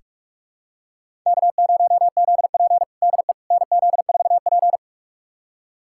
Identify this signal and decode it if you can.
Morse code, heard as a keyed tone, about 42 wpm, 700 Hz. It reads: K0ZP DE NZ4P
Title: USA Callsign DE Callsign - 4 Characters - 45wpm